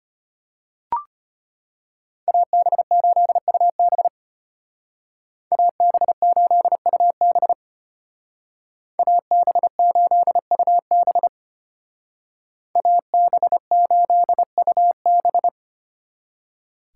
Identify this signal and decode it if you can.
Morse code, heard as a keyed tone, about 30 wpm, 700 Hz. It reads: AB8UB AB8UB AB8UB AB8UB
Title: Speed Racing - USA Call Signs 25wpm